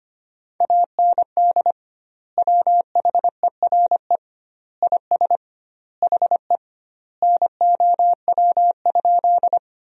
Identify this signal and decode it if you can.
Morse code, heard as a keyed tone, about 25 wpm, 700 Hz. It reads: AND WHERE IS HE NOW?
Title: 2x Word-Spacing - Sentences from Top 200 Words - 25wpm